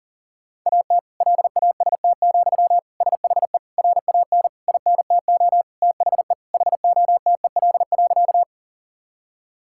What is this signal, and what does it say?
Morse code, heard as a keyed tone, about 37 wpm, 700 Hz. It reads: AT LAST, SHE RAN INTO THE HOTEL.
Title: Sentences from Top 1600 Words 40wpm